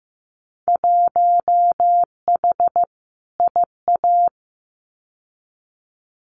Morse code, heard as a keyed tone, about 15 wpm, 700 Hz. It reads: 1H IA
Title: Winter Field Day Exchanges 15wpm